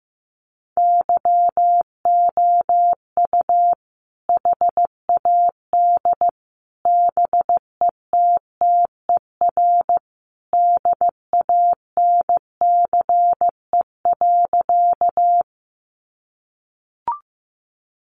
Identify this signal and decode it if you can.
Morse code, heard as a keyed tone, about 15 wpm, 700 Hz. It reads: YOU HAD BETTER DANCE.
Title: Sentences from Top 700 Words 15wpm